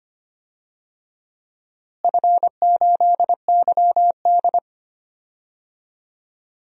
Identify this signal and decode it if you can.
Morse code, heard as a keyed tone, about 25 wpm, 700 Hz. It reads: F8YD
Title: DXCC-100 Call Signs - 4 Characters - Repeated 2x - 25wpm